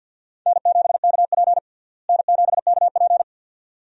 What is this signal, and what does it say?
Morse code, heard as a keyed tone, about 53 wpm, 690 Hz. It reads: N7KP N7KP
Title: USA Call Signs - 4 Characters - Repeated 2x - 50wpm